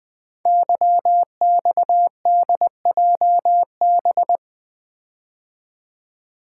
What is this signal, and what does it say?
Morse code, heard as a keyed tone, about 20 wpm, 700 Hz. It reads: YXDJB
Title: Code Groups - 5 Letters - 20wpm